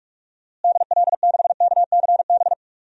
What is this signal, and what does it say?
Morse code, heard as a keyed tone, about 45 wpm, 700 Hz. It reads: DL6KCB
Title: DXCC-100 Call Signs - 6 Characters - Repeated 3x - 45wpm